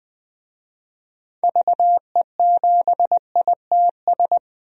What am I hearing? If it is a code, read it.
Morse code, heard as a keyed tone, about 20 wpm, 700 Hz. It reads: VE7ITS
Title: DXCC-100 Call Signs - 6 Characters - 20wpm